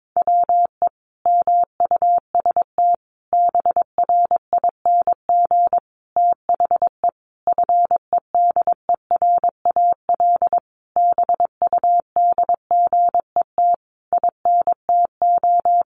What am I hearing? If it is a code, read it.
Morse code, heard as a keyed tone, about 22 wpm, 700 Hz. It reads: WE MUST BRING THE FEDERAL BUDGET INTO
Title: Sentences from Top 1200 Words 22wpm